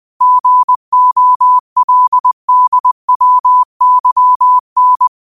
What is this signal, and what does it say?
Morse code, heard as a keyed tone, about 20 wpm, 1000 Hz. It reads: GOLDWYN
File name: qotdfiles / 20wpm